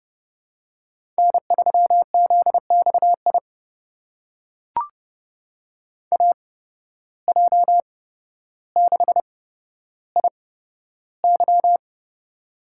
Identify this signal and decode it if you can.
Morse code, heard as a keyed tone, about 30 wpm, spaced as for 13 wpm, 700 Hz. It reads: N3ZXI AJ6IY
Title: USA Call Signs - 5 Characters - 8x Character Spacing - 30wpm